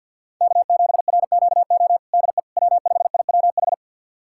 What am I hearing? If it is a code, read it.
Morse code, heard as a keyed tone, about 46 wpm, 700 Hz. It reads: K7RQO DE W5IWH
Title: USA Callsign DE Callsign - 5 Characters - 50wpm